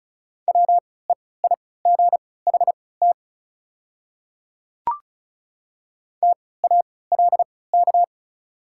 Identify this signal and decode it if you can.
Morse code, heard as a keyed tone, about 34 wpm, spaced as for 20 wpm, 700 Hz. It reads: WEIGHT TALK
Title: 3x Character Spacing - Top 500 Words - 35wpm